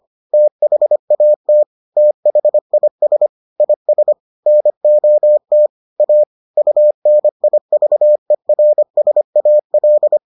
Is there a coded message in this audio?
Yes: Morse code from 0.3 to 10.2 s, heard as a keyed tone, about 25 wpm, 600 Hz. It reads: THAT THIS IS NOT A UNIVERSAL